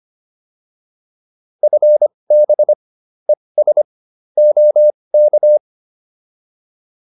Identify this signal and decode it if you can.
Morse code, heard as a keyed tone, about 25 wpm, spaced as for 20 wpm, 600 Hz. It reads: FB ES OK